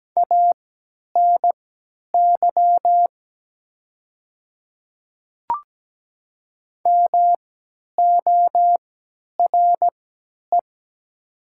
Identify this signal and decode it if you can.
Morse code, heard as a keyed tone, about 17 wpm, spaced as for 10 wpm, 700 Hz. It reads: ANY MORE
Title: 3x Character Spacing - Top 300 Words - 17wpm